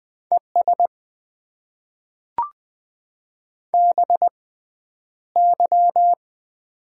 Morse code, heard as a keyed tone, about 20 wpm, spaced as for 10 wpm, 700 Hz. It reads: ES BY
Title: Top 90 QSO Elements - 6x Character Spacing - 20wpm